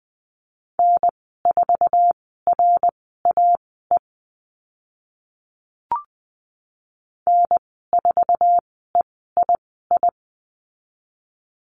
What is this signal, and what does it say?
Morse code, heard as a keyed tone, about 20 wpm, spaced as for 14 wpm, 700 Hz. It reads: N4RAE N4EII
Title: 076 Lesson 13g - Letter L - Call Signs 20wpm